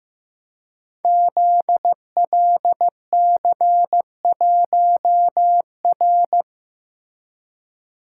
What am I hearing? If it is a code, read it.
Morse code, heard as a keyed tone, about 15 wpm, 700 Hz. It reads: ZLC1R
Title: Code Groups - 5 Letter:Numbers - 15wpm